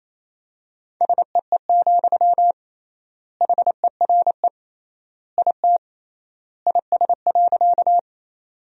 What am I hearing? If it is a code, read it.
Morse code, heard as a keyed tone, about 28 wpm, 700 Hz. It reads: SEE, HERE IT IS.
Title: 3x Word-Spacing - Sentences from Top 200 Words - 28wpm